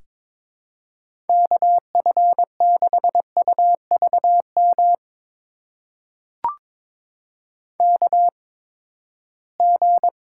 Morse code, heard as a keyed tone, about 22 wpm, spaced as for 15 wpm, 700 Hz. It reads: KF6UVM KG
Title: USA Call Signs - 6 Characters - 8x Character Spacing - 22wpm